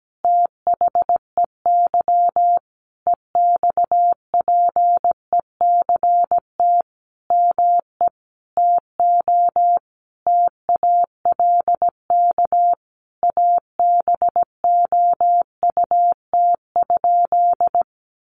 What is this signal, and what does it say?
Morse code, heard as a keyed tone, about 17 wpm, 700 Hz. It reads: THEY EXPECT ME TO TALK ABOUT?
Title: Sentences from Top 1100 Words 17wpm